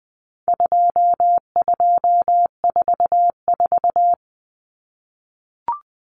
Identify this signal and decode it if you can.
Morse code, heard as a keyed tone, about 20 wpm, 700 Hz. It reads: 2244
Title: Common Cut Numbers - 4-Digit 20wpm